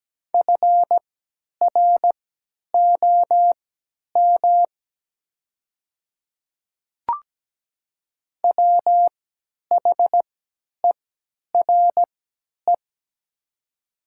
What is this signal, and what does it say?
Morse code, heard as a keyed tone, about 17 wpm, spaced as for 10 wpm, 700 Hz. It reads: FROM WHERE
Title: 3x Character Spacing - Top 500 Words - 17wpm